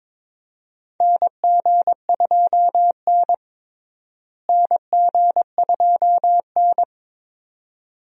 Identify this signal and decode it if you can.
Morse code, heard as a keyed tone, about 22 wpm, 700 Hz. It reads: NG2N NG2N
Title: USA Call Signs - 4 Characters - Repeated 2x - 22wpm